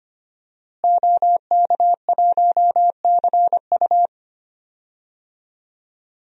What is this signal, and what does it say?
Morse code, heard as a keyed tone, about 25 wpm, 700 Hz. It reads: OK1CU
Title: DXCC-100 Call Signs - 5 Characters - Repeated 2x - 25wpm